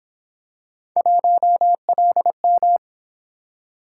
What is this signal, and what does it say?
Morse code, heard as a keyed tone, about 27 wpm, 700 Hz. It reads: 1LM
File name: Speed-Racing - Three Letter-Number Combinations - 22wpm